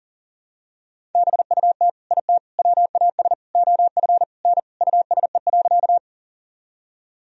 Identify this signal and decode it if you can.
Morse code, heard as a keyed tone, about 40 wpm, 700 Hz. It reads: BUT IT WAS OF N USE.
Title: Sentences from Top 100 Words and Abbreviations 40wpm